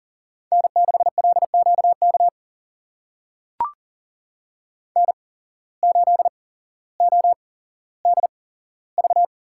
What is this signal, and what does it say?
Morse code, heard as a keyed tone, about 40 wpm, spaced as for 17 wpm, 700 Hz. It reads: N6LQK N8ODV
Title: USA Call Signs - 8x Character Spacing - 40wpm